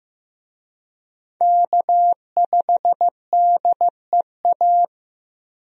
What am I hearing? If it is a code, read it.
Morse code, heard as a keyed tone, about 15 wpm, 700 Hz. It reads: K5DEA
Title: USA Call Signs - 5 Characters - Repeated 2x - 15wpm